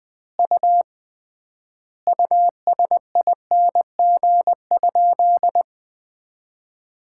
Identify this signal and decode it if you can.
Morse code, heard as a keyed tone, about 20 wpm, 700 Hz. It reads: U USING?
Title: Common QSO Phrases - 3x Word-Spacing - 20wpm